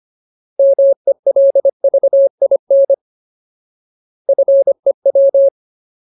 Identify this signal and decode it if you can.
Morse code, heard as a keyed tone, about 25 wpm, 550 Hz. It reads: MELVIN FEW
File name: Ultimate Word List / 25WPM Ultimate